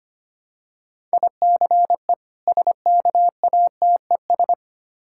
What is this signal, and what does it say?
Morse code, heard as a keyed tone, about 25 wpm, 690 Hz. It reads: ICE SKATES
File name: practice / 25wpm / Difficult Words with Hints - 25wpm